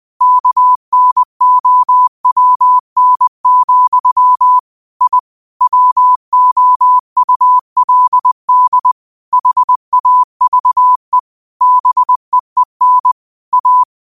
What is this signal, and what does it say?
Morse code, heard as a keyed tone, about 20 wpm, 1000 Hz. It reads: KNOWN, I WOULD HAVE BEEN A